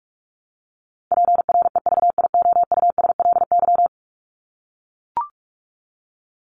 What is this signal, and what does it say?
Morse code, heard as a keyed tone, about 44 wpm, 700 Hz. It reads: PREVIOUSLY
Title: LY Suffix - Top 100 Words with Base Words 45wpm